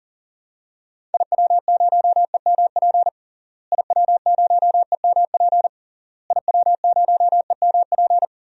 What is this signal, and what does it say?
Morse code, heard as a keyed tone, about 40 wpm, 700 Hz. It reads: IW0EMP IW0EMP IW0EMP
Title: DXCC-100 Call Signs - 6 Characters - Repeated 3x - 40wpm